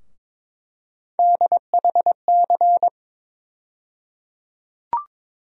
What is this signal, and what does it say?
Morse code, heard as a keyed tone, about 21 wpm, 700 Hz. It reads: DHC
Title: USA Call Sign Postfixes - 3 Letters 22wpm